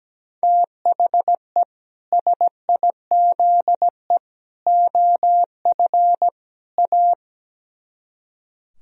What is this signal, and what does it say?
Morse code, heard as a keyed tone, about 17 wpm, 700 Hz. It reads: THE SIZE OF A
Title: Speed-Racing - Sets of 4 Words 17wpm